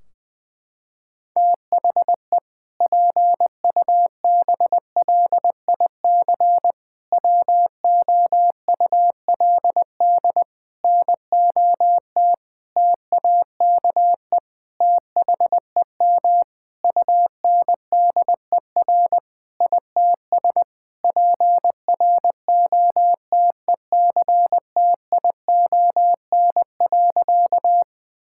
Morse code, heard as a keyed tone, about 20 wpm, 700 Hz. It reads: THE PUBLIC WOULD NOT TAKE THEM UNDER ITS PROTECTION.
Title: Sentences from Top 1600 Words 20wpm